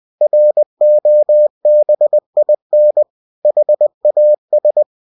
Morse code, heard as a keyed tone, about 20 wpm, 600 Hz. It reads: ROBIN HAS
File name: cw-books / winnie-the-pooh / winnie-20wpm-page-0005